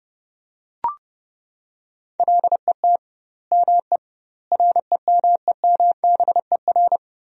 Morse code, heard as a keyed tone, about 30 wpm, 700 Hz. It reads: LET ME REMEMBER
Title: 2x Word-Spacing - Sentences from Top 400 Words - 30wpm